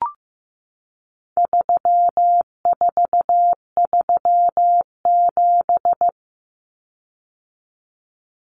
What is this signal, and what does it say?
Morse code, heard as a keyed tone, about 15 wpm, 700 Hz. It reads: 3437
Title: Common Cut Numbers - 4-Digit 15wpm